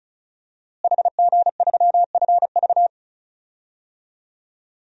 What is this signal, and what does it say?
Morse code, heard as a keyed tone, about 34 wpm, 700 Hz. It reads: HG3FV
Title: DXCC-100 Call Signs - 5 Characters - Repeated 3x - 35wpm